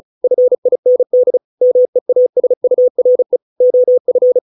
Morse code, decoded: FIND MEASURE OF

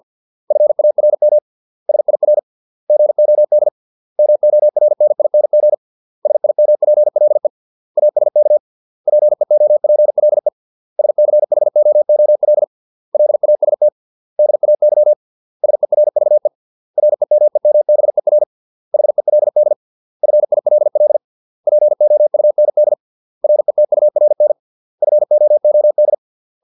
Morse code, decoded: FARM SIR GOD MORNING SIMPLE ASK PEOPLE SCHOOL LAST DAY SERVE REMEMBER HELD FILL POUND RETURN FOOD

